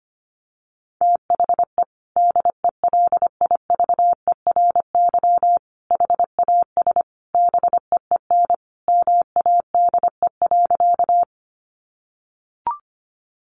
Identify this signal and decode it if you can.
Morse code, heard as a keyed tone, about 25 wpm, 700 Hz. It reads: THE DELIVERY HAS BEEN MADE.
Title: Sentences from Top 1500 Words 25wpm